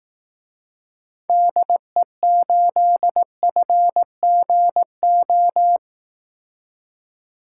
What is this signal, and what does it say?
Morse code, heard as a keyed tone, about 18 wpm, 700 Hz. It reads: DE8FGO
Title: Speed Racing - USA 2x3 Call Signs 15wpm